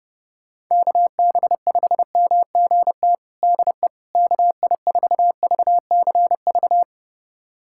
Morse code, heard as a keyed tone, about 32 wpm, 700 Hz. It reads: KB5MGT DE KI4VCV
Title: USA Callsign DE Callsign - 6 Characters - 30wpm